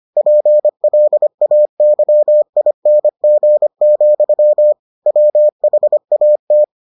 Morse code, heard as a keyed tone, about 25 wpm, 600 Hz. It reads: PLAYING, WHAT